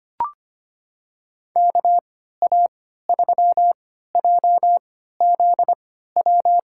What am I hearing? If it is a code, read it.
Morse code, heard as a keyed tone, about 25 wpm, spaced as for 14 wpm, 700 Hz. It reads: KA3JZW
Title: USA Call Signs - 6 Characters - 3x Character Spacing - 25wpm